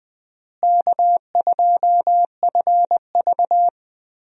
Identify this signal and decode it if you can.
Morse code, heard as a keyed tone, about 20 wpm, 700 Hz. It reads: K2FV
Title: USA Call Signs - 4 Characters - Repeated 3x - 20wpm